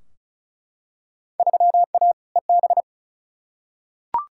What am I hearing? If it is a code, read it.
Morse code, heard as a keyed tone, about 36 wpm, 700 Hz. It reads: 3A EB
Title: Field Day Exchanges Encore - 35wpm